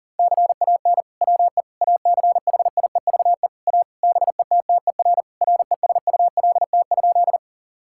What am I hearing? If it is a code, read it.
Morse code, heard as a keyed tone, about 37 wpm, 700 Hz. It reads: CAN WE ACHIEVE A BETTER RESULT?
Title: Sentences from Top 1000 Words - Review Part 6 - 40wpm